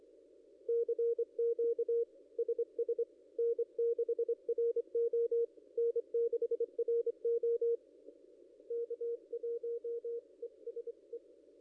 Morse code, heard as a keyed tone, about 25 wpm, 470 Hz. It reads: CQ SS N6RO N6RO K1ESE
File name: contests / ss-cw-k5tr-op-n5rz -0800